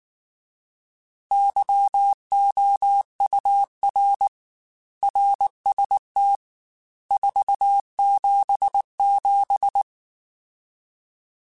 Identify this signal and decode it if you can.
Morse code, heard as a keyed tone, about 19 wpm, 800 Hz. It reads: YOUR RST 477
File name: RSTs / RST-mp3s / Your RSTs @18